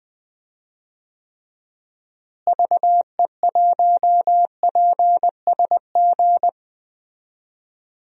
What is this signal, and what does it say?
Morse code, heard as a keyed tone, about 20 wpm, 700 Hz. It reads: VE1PSG
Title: DXCC-100 Call Signs - 6 Characters - Repeated 3x - 20wpm